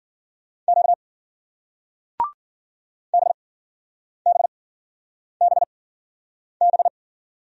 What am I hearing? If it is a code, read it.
Morse code, heard as a keyed tone, about 48 wpm, spaced as for 9 wpm, 700 Hz. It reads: Q BBBB
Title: Speed-Racing Single Letters 40wpm